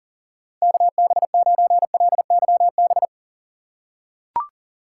Morse code, heard as a keyed tone, about 40 wpm, 700 Hz. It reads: KB9LYB
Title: USA Call Signs - 6x Character Spacing - 40wpm